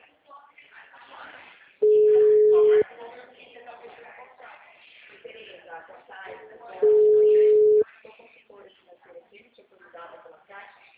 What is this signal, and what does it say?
Ringback tone (Europe, 425 Hz), with speech.